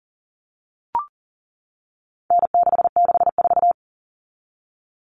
Morse code, heard as a keyed tone, about 40 wpm, 700 Hz. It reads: N664
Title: All Cut Numbers - 4-Digit 40wpm